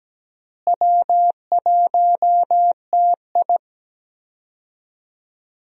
Morse code, heard as a keyed tone, about 17 wpm, 700 Hz. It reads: W1TI
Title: USA Call Signs - 4 Characters - Repeated 2x - 17wpm